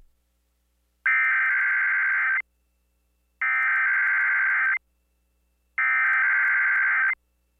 Emergency Alert System (SAME) tones (header bursts), heard as harsh buzzing bursts. It is heard from 1.1 to 7.1 s.